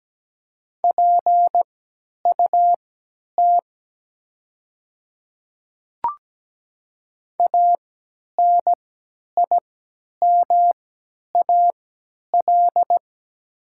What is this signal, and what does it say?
Morse code, heard as a keyed tone, about 17 wpm, spaced as for 10 wpm, 700 Hz. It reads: PUT ANIMAL
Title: 3x Character Spacing - Top 200 Words - 17wpm